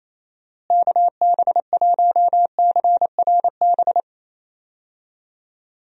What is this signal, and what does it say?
Morse code, heard as a keyed tone, about 28 wpm, 700 Hz. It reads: KB1CRB
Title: USA Call Signs - 6 Characters - Repeated 2x - 28wpm